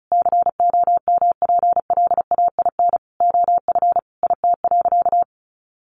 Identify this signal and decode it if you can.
Morse code, heard as a keyed tone, about 34 wpm, 700 Hz. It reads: COMPLAIN OF IT.
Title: Sentences from Top 1300 Words 35wpm